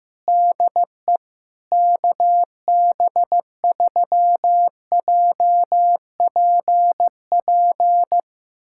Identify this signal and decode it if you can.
Morse code, heard as a keyed tone, about 15 wpm, 700 Hz. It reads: DE KB3JPP
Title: USA Callsign DE Callsign - 6 Characters - 15wpm